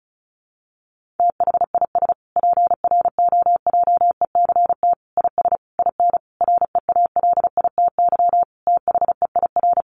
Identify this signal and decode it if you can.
Morse code, heard as a keyed tone, about 34 wpm, 700 Hz. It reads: THIS PROJECT IS IN REALITY THEIR
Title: Sentences from Top 1400 Words 35wpm